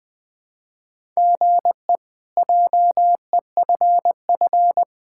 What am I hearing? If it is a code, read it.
Morse code, heard as a keyed tone, about 20 wpm, 700 Hz. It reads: GE JEFF
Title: K1USN SST QSO Practice - 20wpm